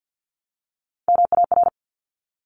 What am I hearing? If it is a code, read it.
Morse code, heard as a keyed tone, about 50 wpm, 700 Hz. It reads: MAR
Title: Top Three Letter Combinations in Words 50wpm